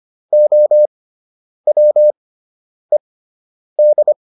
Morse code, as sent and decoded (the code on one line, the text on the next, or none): --- .-- . -..
OWED